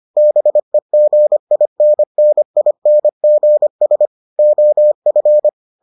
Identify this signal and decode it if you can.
Morse code, heard as a keyed tone, about 25 wpm, 600 Hz. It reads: BEGINNINGS OF